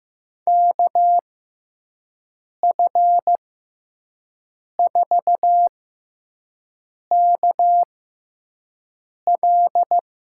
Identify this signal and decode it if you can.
Morse code, heard as a keyed tone, about 15 wpm, spaced as for 5 wpm, 700 Hz. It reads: KF4KL
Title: USA Call Signs - 6 Characters - 6x Character Spacing - 15wpm